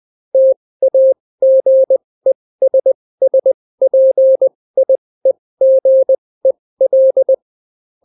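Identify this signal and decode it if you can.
Morse code, heard as a keyed tone, about 20 wpm, spaced as for 16 wpm, 530 Hz. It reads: TAGESSPIEGEL
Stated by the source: Daily Morse weather and top stories at 16 wpm, clean without QRN.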